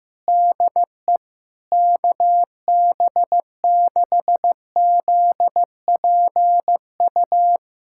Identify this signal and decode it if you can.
Morse code, heard as a keyed tone, about 16 wpm, 700 Hz. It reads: DE KB6ZPU
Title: USA Callsign DE Callsign 15wpm